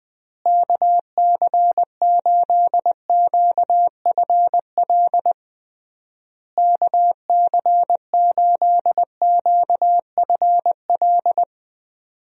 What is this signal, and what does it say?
Morse code, heard as a keyed tone, about 20 wpm, 700 Hz. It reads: KC8QFL KC8QFL
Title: USA Call Signs - 6 Characters - Repeated 3x - 20wpm